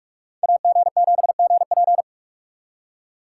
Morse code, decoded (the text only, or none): AG7GP